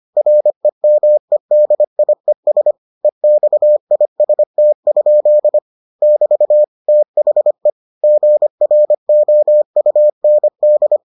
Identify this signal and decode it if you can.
Morse code, heard as a keyed tone, about 25 wpm, 600 Hz. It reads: REMEDIES EXIST? <BT> THE GROUND